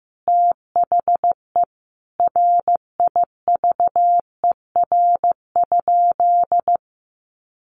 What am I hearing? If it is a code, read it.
Morse code, heard as a keyed tone, about 15 wpm, 700 Hz. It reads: THE RIVER?
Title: Sentences from Top 300 Words 15wpm